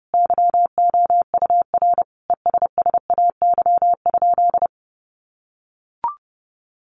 Morse code, heard as a keyed tone, about 30 wpm, 700 Hz. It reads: YOUR ESSAY?
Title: Sentences from Top 1500 Words 30wpm